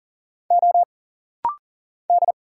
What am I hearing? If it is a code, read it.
Morse code, heard as a keyed tone, about 40 wpm, 700 Hz. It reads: O D
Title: Single Letter-Number Alt - Rapid-Fire 40wpm